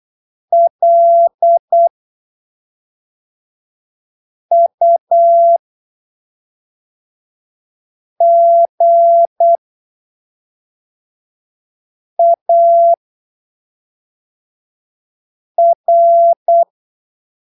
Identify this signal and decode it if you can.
Morse code, heard as a keyed tone, about 8 wpm, spaced as for 3 wpm, 680 Hz. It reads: LUGAR
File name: arquivos / texto / texto 28 08ppm